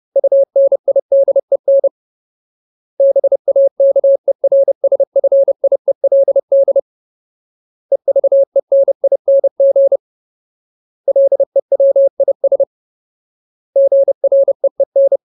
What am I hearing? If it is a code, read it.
Morse code, heard as a keyed tone, about 30 wpm, 550 Hz. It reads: UNIDEN BAKERSFIELD EVENING LEWIS GREEN
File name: Ultimate Word List / 30WPM Ultimate